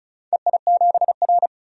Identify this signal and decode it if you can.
Morse code, heard as a keyed tone, about 34 wpm, 700 Hz. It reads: EI7R
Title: DXCC-100 Call Signs - 4 Characters - 35wpm